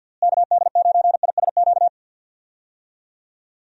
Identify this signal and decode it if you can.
Morse code, heard as a keyed tone, about 53 wpm, 700 Hz. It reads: KD9ISQ
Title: USA Call Signs - 6 Characters - Repeated 3x - 50wpm